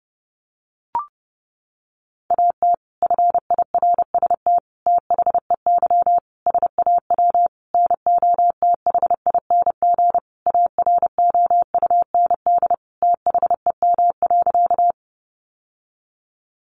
Morse code, heard as a keyed tone, about 30 wpm, 700 Hz. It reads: AT FIRST THEY SAW NOTHING AROUND THEM.
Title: Sentences from Top 600 Words 30wpm